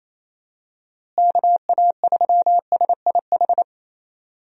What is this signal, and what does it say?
Morse code, heard as a keyed tone, about 28 wpm, 700 Hz. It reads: KA3SIH
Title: USA Call Signs - 6 Characters - Repeated 3x - 28wpm